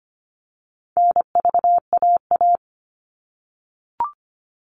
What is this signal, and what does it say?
Morse code, heard as a keyed tone, about 25 wpm, 700 Hz. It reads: NVAA
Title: All Cut Numbers - 4-Digit 25wpm